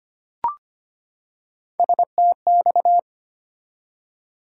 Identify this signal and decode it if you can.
Morse code, heard as a keyed tone, about 25 wpm, 700 Hz. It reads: STX
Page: morse-code-ninja-practice-25wpm